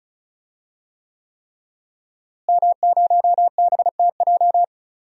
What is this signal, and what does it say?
Morse code, heard as a keyed tone, about 36 wpm, 700 Hz. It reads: M0BTJ
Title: DXCC-100 Call Signs - 5 Characters - Repeated 3x - 35wpm